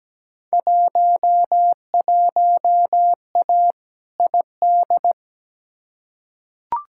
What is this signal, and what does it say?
Morse code, heard as a keyed tone, about 17 wpm, 700 Hz. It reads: 11A ID
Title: Field Day Exchanges Encore - 17wpm